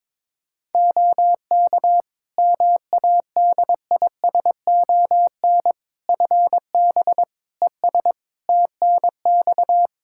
Morse code, heard as a keyed tone, about 22 wpm, 700 Hz. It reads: OK MADISON FB ES TNX
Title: Practice QSO - Exchange 2 - Info - 22wpm